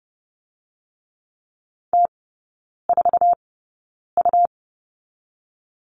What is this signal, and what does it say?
Morse code, heard as a keyed tone, about 30 wpm, spaced as for 9 wpm, 700 Hz. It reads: T4U
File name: Lesson 17e - Letter u - Three Characters 30wpm